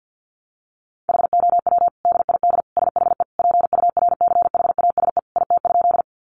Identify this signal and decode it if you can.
Morse code, heard as a keyed tone, about 50 wpm, 700 Hz. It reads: HOW DID SHE PURCHASE IT?
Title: Sentences from Top 800 Words 50wpm